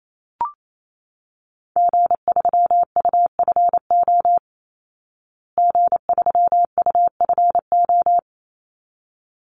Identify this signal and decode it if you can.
Morse code, heard as a keyed tone, about 28 wpm, 700 Hz. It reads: G3UFO G3UFO
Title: Call Signs Repeated 3x - 28wpm